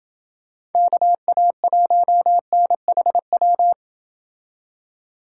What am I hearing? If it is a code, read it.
Morse code, heard as a keyed tone, about 28 wpm, 700 Hz. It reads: KA1NHW